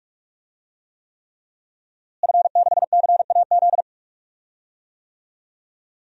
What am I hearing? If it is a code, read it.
Morse code, heard as a keyed tone, about 46 wpm, 700 Hz. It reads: F6CAZ
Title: DXCC-100 Call Signs - 5 Characters - Repeated 3x - 45wpm